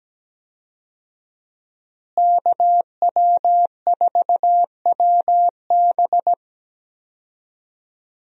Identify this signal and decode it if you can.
Morse code, heard as a keyed tone, about 17 wpm, 700 Hz. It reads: KW4WB
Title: USA Call Signs - 5 Characters - Repeated 2x - 17wpm